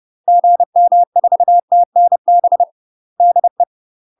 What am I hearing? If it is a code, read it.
Morse code, heard as a keyed tone, about 30 wpm, 700 Hz. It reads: GM4TNB DE
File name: UK MASTER QSOs TRAINING/30WPM